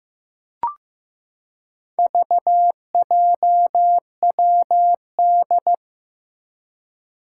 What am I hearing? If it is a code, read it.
Morse code, heard as a keyed tone, about 15 wpm, 700 Hz. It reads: VJWD
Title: Four Challenging Letter-Number Combinations 15wpm